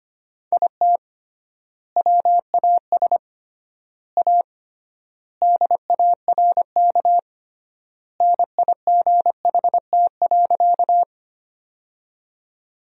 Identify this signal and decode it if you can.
Morse code, heard as a keyed tone, about 25 wpm, 700 Hz. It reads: IT WAS A DARK NIGHT.
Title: 3x Word-Spacing - Sentences from Top 500 Words - 25wpm